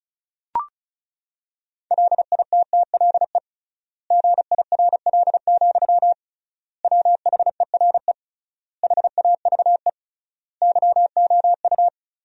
Morse code, heard as a keyed tone, about 34 wpm, 700 Hz. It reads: LITTLE GIRL, WHERE HAVE YOU
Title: 3x Word-Spacing - Sentences from Top 400 Words - 35wpm